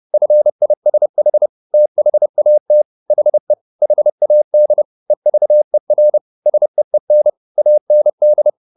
Morse code, heard as a keyed tone, about 30 wpm, 600 Hz. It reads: FISH THAT HE HAD EVER SEEN AND
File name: the old man and the sea / the old man and the sea 1 30wpm